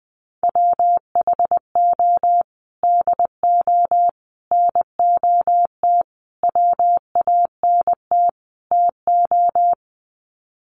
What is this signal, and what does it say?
Morse code, heard as a keyed tone, about 20 wpm, 700 Hz. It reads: WHO DO NOT WANT TO
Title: Sets of 5 Words Encore2 20wpm